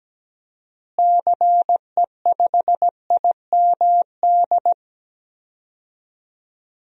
Morse code, heard as a keyed tone, about 17 wpm, 700 Hz. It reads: CE5IMD
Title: DXCC-100 Call Signs - 6 Characters - Repeated 2x - 17wpm